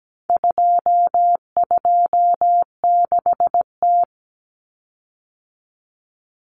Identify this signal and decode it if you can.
Morse code, heard as a keyed tone, about 17 wpm, 700 Hz. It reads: 226T
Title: Common Cut Numbers - 4-Digit 17wpm